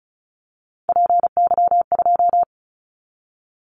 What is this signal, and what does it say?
Morse code, heard as a keyed tone, about 34 wpm, 700 Hz. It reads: PY2
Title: Callsign Prefixes for Top 50 DXCC - 35wpm